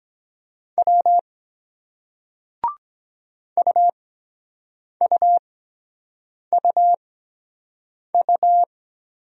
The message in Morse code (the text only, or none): W UUUU